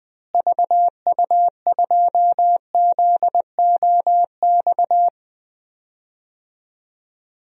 Morse code, heard as a keyed tone, about 20 wpm, 700 Hz. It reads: VU2ZOX
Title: DXCC-100 Call Signs - 6 Characters - Repeated 2x - 20wpm